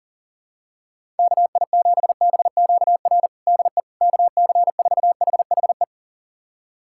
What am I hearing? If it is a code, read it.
Morse code, heard as a keyed tone, about 40 wpm, 700 Hz. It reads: KI7BQR DE KC4HHE
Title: USA Callsign DE Callsign 40wpm